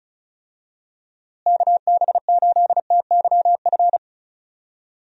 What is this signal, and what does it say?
Morse code, heard as a keyed tone, about 36 wpm, 700 Hz. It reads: KB8TYF
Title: USA Call Signs - 6 Characters - Repeated 2x - 35wpm